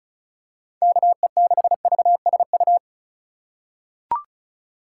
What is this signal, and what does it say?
Morse code, heard as a keyed tone, about 36 wpm, 700 Hz. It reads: KE6VSU
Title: USA Call Signs - 6 Characters - 4x Character Spacing - 35wpm